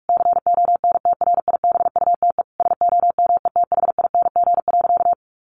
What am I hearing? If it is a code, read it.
Morse code, heard as a keyed tone, about 44 wpm, 700 Hz. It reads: CONTRIBUTE SOMETHING.
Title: Sentences from Top 1400 Words 45wpm